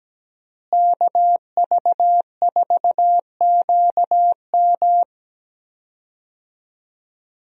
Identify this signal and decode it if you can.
Morse code, heard as a keyed tone, about 17 wpm, 700 Hz. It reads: KV4QM
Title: USA Call Signs - 5 Characters - Repeated 2x - 17wpm